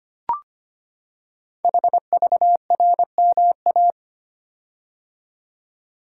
Morse code, heard as a keyed tone, about 25 wpm, 700 Hz. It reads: HVRMA
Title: Code Groups - 5 Letters - 25wpm